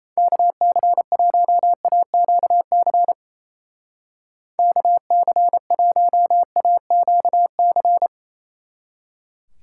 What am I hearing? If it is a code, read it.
Morse code, heard as a keyed tone, about 31 wpm, 700 Hz. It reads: KC1AQC KC1AQC
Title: Speed Racing - USA Call Signs 28wpm